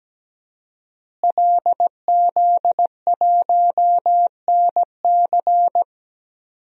Morse code, heard as a keyed tone, about 17 wpm, 700 Hz. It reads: LZ1NC
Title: DXCC-100 Call Signs - 5 Characters - Repeated 2x - 17wpm